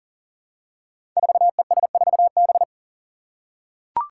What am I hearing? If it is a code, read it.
Morse code, heard as a keyed tone, about 40 wpm, 700 Hz. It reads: 4ES4B
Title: Five Challenging Letter-Number Combinations 40wpm